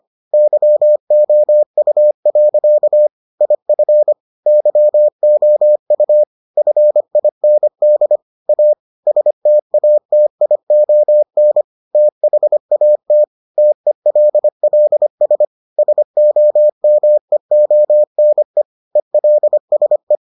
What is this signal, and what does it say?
Morse code, heard as a keyed tone, about 25 wpm, 600 Hz. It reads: YOU. IF YOU FIND A STATION THAT TELLS SOMEONE ELSE